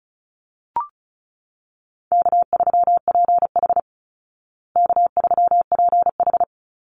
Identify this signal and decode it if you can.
Morse code, heard as a keyed tone, about 34 wpm, 700 Hz. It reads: K3PH K3PH
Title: Easy Call Signs Repeated 3x - 35wpm